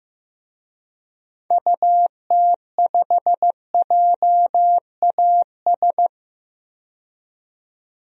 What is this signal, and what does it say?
Morse code, heard as a keyed tone, about 15 wpm, 700 Hz. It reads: UT5JAS
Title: DXCC-100 Call Signs - 6 Characters - 15wpm